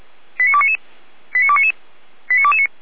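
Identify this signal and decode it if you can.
Five-tone selective call (ZVEI), heard as quick runs of five tones, tones of about 70 ms. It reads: 89299 89299 89229
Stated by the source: ZVEI five-tone selective call 89337 received on SDR, from fire brigade Feuerwehr Orlamünde.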